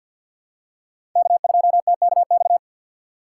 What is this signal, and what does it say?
Morse code, heard as a keyed tone, about 50 wpm, 700 Hz. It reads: K2TKX
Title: USA Call Signs - 4x Character Spacing - 50wpm